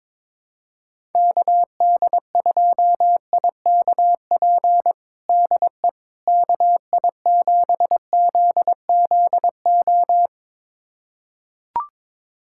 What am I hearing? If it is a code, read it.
Morse code, heard as a keyed tone, about 22 wpm, 700 Hz. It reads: KD2IKP DE KI7ZZO
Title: USA Callsign DE Callsign - 6 Characters - 22wpm